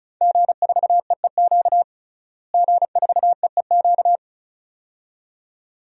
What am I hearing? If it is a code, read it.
Morse code, heard as a keyed tone, about 36 wpm, 700 Hz. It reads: G4EEQ G4EEQ
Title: DXCC-100 Call Signs - 5 Characters - Repeated 2x - 35wpm